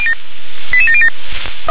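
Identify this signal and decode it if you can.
Five-tone selective call (ZVEI), heard as quick runs of five tones, tones of about 70 ms. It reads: 88707
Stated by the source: ZVEI five-tone selective call received by SDR.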